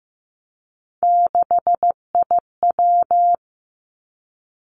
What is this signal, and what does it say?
Morse code, heard as a keyed tone, about 15 wpm, 700 Hz. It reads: 6IW
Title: Three Challenging Letter-Number Combinations 15wpm